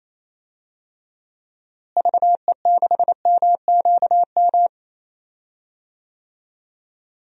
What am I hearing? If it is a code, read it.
Morse code, heard as a keyed tone, about 28 wpm, 700 Hz. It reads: VE6MQM
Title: DXCC-100 Call Signs - 6 Characters - Repeated 2x - 28wpm